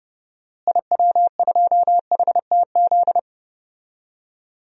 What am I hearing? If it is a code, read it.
Morse code, heard as a keyed tone, about 30 wpm, 700 Hz. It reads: IW2HTZ
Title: DXCC-100 Call Signs - 6 Characters - Repeated 3x - 30wpm